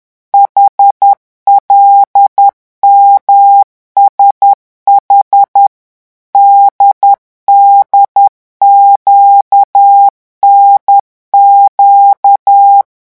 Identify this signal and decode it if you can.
Morse code, heard as a keyed tone, about 10 wpm, 800 Hz. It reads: HLMSH DDQNQ